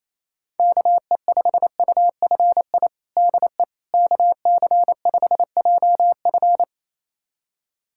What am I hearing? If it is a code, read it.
Morse code, heard as a keyed tone, about 28 wpm, 700 Hz. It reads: KE5UFI DE KC5JF
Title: USA Callsign DE Callsign 28wpm